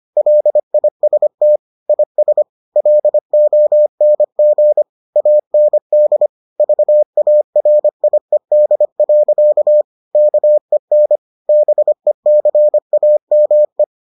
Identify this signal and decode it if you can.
Morse code, heard as a keyed tone, about 25 wpm, 600 Hz. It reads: LIST IS LONG AND VARIED. KEN BECAME